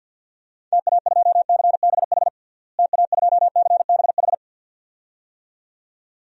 Morse code, heard as a keyed tone, about 47 wpm, 700 Hz. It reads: TA2CBH TA2CBH
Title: DXCC-100 Call Signs - 6 Characters - Repeated 2x - 50wpm